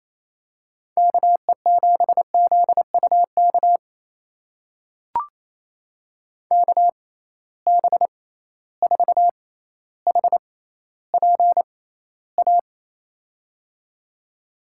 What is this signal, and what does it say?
Morse code, heard as a keyed tone, about 28 wpm, spaced as for 14 wpm, 700 Hz. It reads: KE7ZUK KB4HPA
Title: USA Call Signs - 6 Characters - 6x Character Spacing - 28wpm